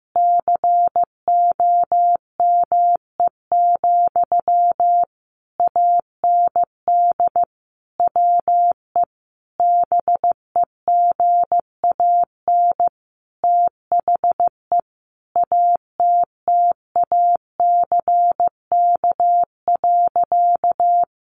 Morse code, heard as a keyed tone, about 15 wpm, 700 Hz. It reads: COME, AND WE BEGAN THE ATTACK.
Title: Sentences from Top 1500 Words 15wpm